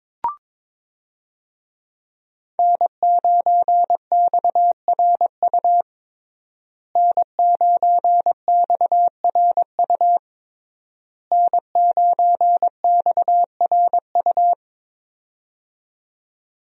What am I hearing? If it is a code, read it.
Morse code, heard as a keyed tone, about 21 wpm, 700 Hz. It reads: N9XRU N9XRU N9XRU
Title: USA Call Signs - 5 Characters - Repeated 3x - 22wpm